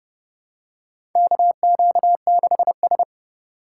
Morse code, heard as a keyed tone, about 30 wpm, 700 Hz. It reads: KQ6S
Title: USA Call Signs - 4 Characters - 4x Character Spacing - 30wpm